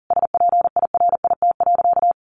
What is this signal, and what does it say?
Morse code, heard as a keyed tone, about 40 wpm, 700 Hz. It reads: SPIRIT.
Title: Sentences from Top 1400 Words 40wpm